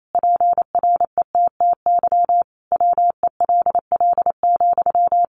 Morse code, heard as a keyed tone, about 28 wpm, 700 Hz. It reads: PRETTY WELL,
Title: Sentences from Top 1000 Words 28wpm